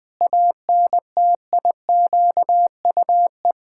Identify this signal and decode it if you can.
Morse code, heard as a keyed tone, about 20 wpm, 700 Hz. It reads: ANTIQUE
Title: Pangram Challenge v2 - 20wpm